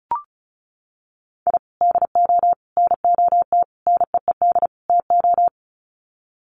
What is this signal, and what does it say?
Morse code, heard as a keyed tone, about 34 wpm, 700 Hz. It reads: I DO NOT NEED TO
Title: Sets of 5 Words 35wpm